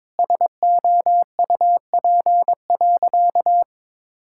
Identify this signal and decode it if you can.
Morse code, heard as a keyed tone, about 21 wpm, 700 Hz. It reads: SOUP.
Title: Sentences from Top 1000 Words - Review Part 6 - 22wpm